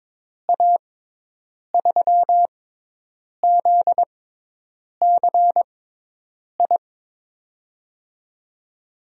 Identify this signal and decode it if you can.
Morse code, heard as a keyed tone, about 22 wpm, spaced as for 8 wpm, 700 Hz. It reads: A3ZCI
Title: USA Call Signs - 6x Character Spacing - 22wpm